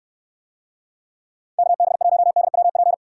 Morse code, heard as a keyed tone, about 64 wpm, 700 Hz. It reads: KB1DWP